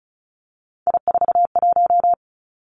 Morse code, heard as a keyed tone, about 36 wpm, 700 Hz. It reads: I41
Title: Three Challenging Letter-Number Combinations 35wpm